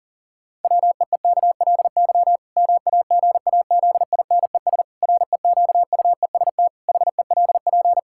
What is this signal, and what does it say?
Morse code, heard as a keyed tone, about 40 wpm, 700 Hz. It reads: WEEKLY MAGAZINES REQUEST HELP
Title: Pangram Challenge v2 - 40wpm